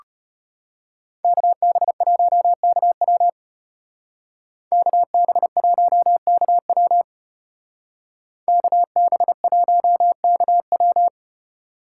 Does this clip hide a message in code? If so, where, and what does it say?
Yes: Morse code from 1.2 to 11.1 s, heard as a keyed tone, about 33 wpm, 700 Hz. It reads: KB1KW KB1KW KB1KW